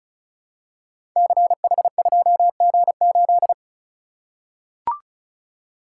Morse code, heard as a keyed tone, about 36 wpm, 700 Hz. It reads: CH2G8
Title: Code Groups - 5 Letter:Numbers - 35wpm